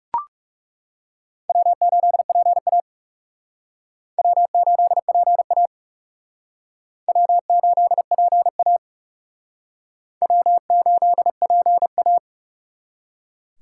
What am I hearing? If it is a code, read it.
Morse code, heard as a keyed tone, about 37 wpm, 700 Hz. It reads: W8PA W8PA W8PA W8PA
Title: Speed Racing - USA 1x2 Call Signs 30wpm